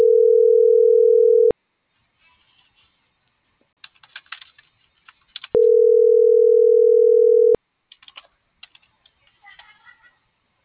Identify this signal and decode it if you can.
Ringback tone (North America), 440 + 480 Hz.